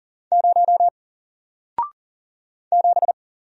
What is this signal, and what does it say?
Morse code, heard as a keyed tone, about 40 wpm, 700 Hz. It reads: 0 7
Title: Single Letter-Number 40wpm